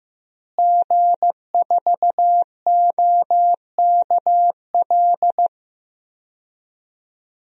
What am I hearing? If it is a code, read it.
Morse code, heard as a keyed tone, about 16 wpm, 700 Hz. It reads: G4OKL
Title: DXCC-100 Call Signs - 5 Characters - Repeated 3x - 15wpm